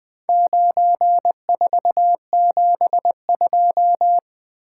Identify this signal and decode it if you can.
Morse code, heard as a keyed tone, about 20 wpm, 700 Hz. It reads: 9472
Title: Four Letter-Number Combinations - 20wpm